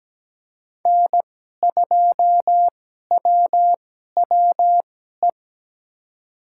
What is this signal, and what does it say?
Morse code, heard as a keyed tone, about 17 wpm, spaced as for 12 wpm, 700 Hz. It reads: N2WWE
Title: USA Call Signs - 5 Characters - 2x Character Spacing - 17wpm